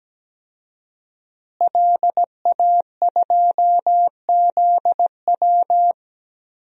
Morse code, heard as a keyed tone, about 17 wpm, 700 Hz. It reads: LA2ZW
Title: DXCC-100 Call Signs - 5 Characters - Repeated 2x - 17wpm